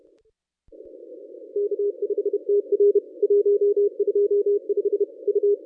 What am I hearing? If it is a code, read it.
Morse code, heard as a keyed tone, about 31 wpm, 410 Hz. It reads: K5TR 125 U
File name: contests / ss-cw-k5tr-op-n5rz -1000